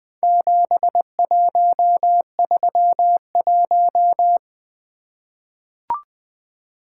Morse code, heard as a keyed tone, about 20 wpm, 700 Hz. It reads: 7131
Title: Four Letter-Number Combinations - 20wpm